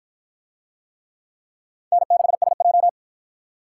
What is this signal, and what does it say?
Morse code, heard as a keyed tone, about 53 wpm, 700 Hz. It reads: N6SJ